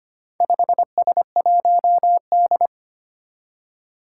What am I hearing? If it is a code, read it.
Morse code, heard as a keyed tone, about 25 wpm, 700 Hz. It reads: 5S1D
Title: Four Challenging Letter-Number Combinations 25wpm